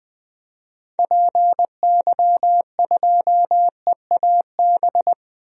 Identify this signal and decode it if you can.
Morse code, heard as a keyed tone, about 20 wpm, 700 Hz. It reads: PY2EAB
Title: DXCC-100 Call Signs - 6 Characters - 20wpm